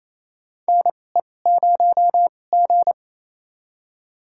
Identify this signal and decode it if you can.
Morse code, heard as a keyed tone, about 28 wpm, spaced as for 20 wpm, 700 Hz. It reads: NE0G
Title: USA Call Signs - 4 Characters - 2x Character Spacing - 28wpm